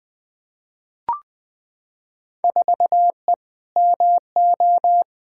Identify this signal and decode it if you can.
Morse code, heard as a keyed tone, about 20 wpm, 700 Hz. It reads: 4E MO
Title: Field Day Exchanges Encore - 20wpm